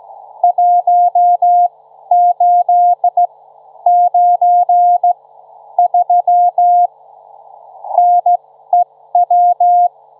Morse code, heard as a keyed tone, about 15 wpm, spaced as for 11 wpm, 710 Hz. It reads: 1893 NEW
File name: Trivia-ZL2WT